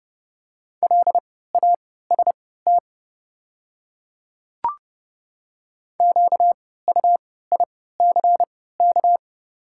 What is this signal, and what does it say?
Morse code, heard as a keyed tone, about 30 wpm, spaced as for 17 wpm, 690 Hz. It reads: LAST QUICK
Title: 3x Character Spacing - Top 500 Words - 30wpm